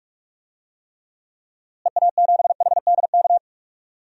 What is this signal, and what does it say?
Morse code, heard as a keyed tone, about 46 wpm, 700 Hz. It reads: EA7HDK
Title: DXCC-100 Call Signs - 6 Characters - Repeated 3x - 45wpm